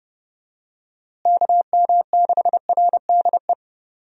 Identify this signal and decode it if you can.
Morse code, heard as a keyed tone, about 32 wpm, 700 Hz. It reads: KM6RDE
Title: USA Call Signs - 6 Characters - Repeated 2x - 30wpm